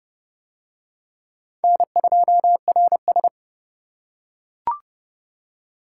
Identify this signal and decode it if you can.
Morse code, heard as a keyed tone, about 29 wpm, 700 Hz. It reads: N2RS